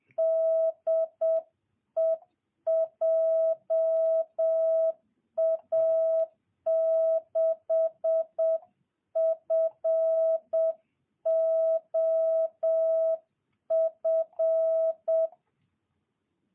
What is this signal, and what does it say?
Morse code, heard as a keyed tone, about 7 wpm, 650 Hz. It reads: DE JA6FOF
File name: jh8cbh / 2022cw